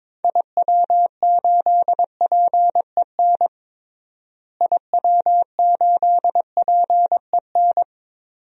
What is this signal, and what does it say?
Morse code, heard as a keyed tone, about 22 wpm, 700 Hz. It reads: IW8PEN IW8PEN
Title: DXCC-100 Call Signs - 6 Characters - Repeated 2x - 22wpm